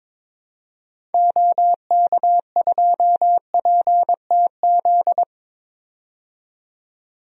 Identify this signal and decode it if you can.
Morse code, heard as a keyed tone, about 22 wpm, 700 Hz. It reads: OK2PTZ
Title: DXCC-100 Call Signs - 6 Characters - 22wpm